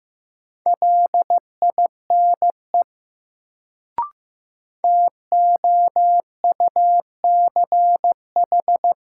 Morse code, heard as a keyed tone, about 15 wpm, 700 Hz. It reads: LINE TOUCH
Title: Sentences from Top 1200 Words - New Word Drill Alt - 15wpm